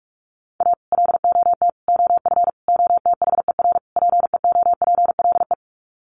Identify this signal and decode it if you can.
Morse code, heard as a keyed tone, about 44 wpm, 700 Hz. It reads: A LOT OF OTHER PEOPLE
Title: Sets of 5 Words Encore2 45wpm